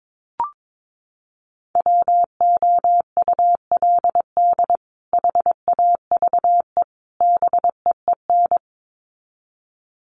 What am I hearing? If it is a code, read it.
Morse code, heard as a keyed tone, about 22 wpm, 700 Hz. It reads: WOULD HAVE BEEN
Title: Sets of 3 Words 22wpm